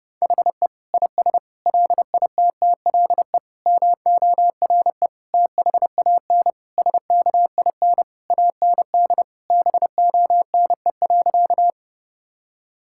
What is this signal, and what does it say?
Morse code, heard as a keyed tone, about 30 wpm, 700 Hz. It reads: HE IS LITTLE MORE THAN SKIN AND BONE.
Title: Sentences from Top 1600 Words 30wpm